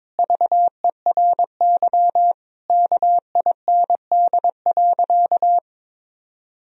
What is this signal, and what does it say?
Morse code, heard as a keyed tone, about 22 wpm, 700 Hz. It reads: VERY KIND.
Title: Sentences from Top 1000 Words - Review Part 3 - 22wpm